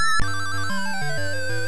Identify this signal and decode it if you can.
There is no code.